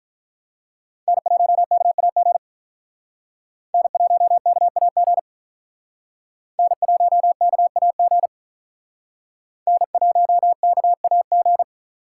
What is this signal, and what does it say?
Morse code, heard as a keyed tone, about 44 wpm, 700 Hz. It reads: N1KAG N1KAG N1KAG N1KAG